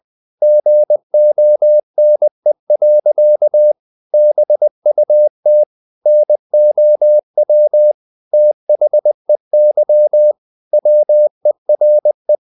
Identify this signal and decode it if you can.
Morse code, heard as a keyed tone, about 20 wpm, 600 Hz. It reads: GONE. BUT NOW THEY WERE
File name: the old man and the sea / the old man and the sea part 1 20wpm